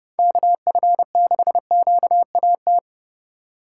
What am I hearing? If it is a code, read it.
Morse code, heard as a keyed tone, about 29 wpm, 700 Hz. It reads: KF6QAT